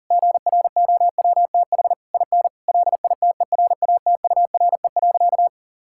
Morse code, heard as a keyed tone, about 40 wpm, 700 Hz. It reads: GROWTH IN LITERATURE.
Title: Sentences from Top 1000 Words - Review Part 5 - 40wpm